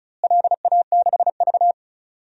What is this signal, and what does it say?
Morse code, heard as a keyed tone, about 34 wpm, 700 Hz. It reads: LA6V
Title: DXCC-100 Call Signs - 4 Characters - Repeated 2x - 35wpm